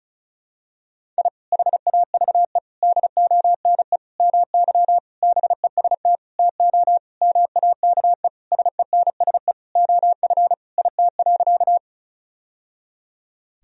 Morse code, heard as a keyed tone, about 36 wpm, 700 Hz. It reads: I HAVE DONE MY BEST TO MAKE SENSE OF IT.
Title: Sentences from Top 1000 Words - Review Part 4 - 35wpm